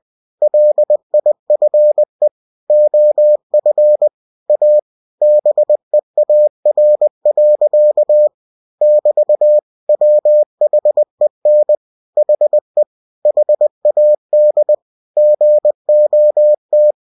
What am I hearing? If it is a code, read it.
Morse code, heard as a keyed tone, about 20 wpm, 600 Hz. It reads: LIFE OF A BEAR. <BT> WHEN HE HAD GOT